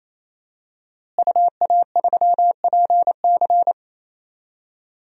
Morse code, heard as a keyed tone, about 28 wpm, 700 Hz. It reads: UA3PC
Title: DXCC-100 Call Signs - 5 Characters - Repeated 3x - 28wpm